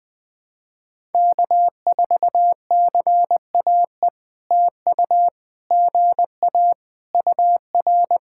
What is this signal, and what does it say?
Morse code, heard as a keyed tone, about 19 wpm, 700 Hz. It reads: K4CAE TU GA UR
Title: POTA QSO Practice - 20wpm